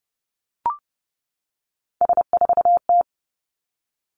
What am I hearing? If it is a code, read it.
Morse code, heard as a keyed tone, about 30 wpm, 700 Hz. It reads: S4T
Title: Three Challenging Letter-Number Combinations 30wpm